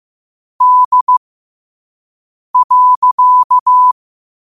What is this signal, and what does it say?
Morse code, heard as a keyed tone, about 15 wpm, spaced as for 5 wpm, 1000 Hz. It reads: D.